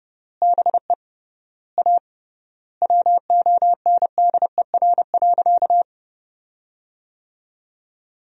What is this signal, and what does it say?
Morse code, heard as a keyed tone, about 30 wpm, 700 Hz. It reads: BE A WONDER.
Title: 3x Word-Spacing - Sentences from Top 500 Words - 30wpm